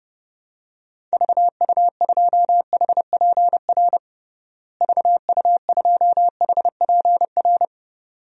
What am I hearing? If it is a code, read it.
Morse code, heard as a keyed tone, about 29 wpm, 700 Hz. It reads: VU2HPR VU2HPR
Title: DXCC-100 Call Signs - 6 Characters - Repeated 2x - 30wpm